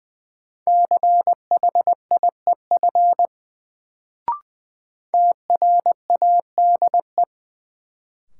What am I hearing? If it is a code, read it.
Morse code, heard as a keyed tone, about 20 wpm, 700 Hz. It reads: CHIEF TRADE
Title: Sets of 5 Words Encore - New Word Drill - 20wpm